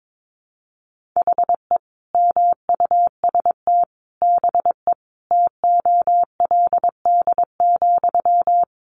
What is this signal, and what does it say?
Morse code, heard as a keyed tone, about 22 wpm, 700 Hz. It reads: HE MUST BE TOLD,
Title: Sentences from Top 1400 Words 22wpm